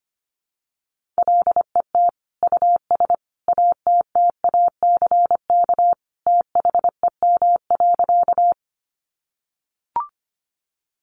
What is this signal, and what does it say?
Morse code, heard as a keyed tone, about 25 wpm, 700 Hz. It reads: LET US ATTACK THEM.
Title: Sentences from Top 1500 Words 25wpm